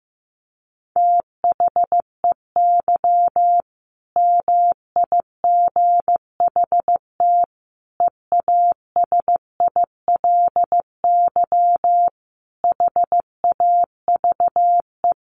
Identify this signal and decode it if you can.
Morse code, heard as a keyed tone, about 15 wpm, 700 Hz. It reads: THEY MIGHT EASILY HAVE
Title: Sentences from Top 1100 Words 15wpm